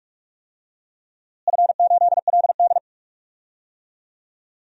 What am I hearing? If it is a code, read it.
Morse code, heard as a keyed tone, about 46 wpm, 700 Hz. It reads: F8LD
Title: DXCC-100 Call Signs - 4 Characters - Repeated 2x - 45wpm